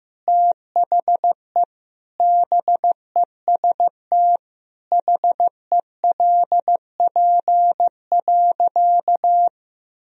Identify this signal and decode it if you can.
Morse code, heard as a keyed tone, about 15 wpm, 700 Hz. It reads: THE BEST HELP.
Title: Sentences from Top 1000 Words - Review Part 1 - 15wpm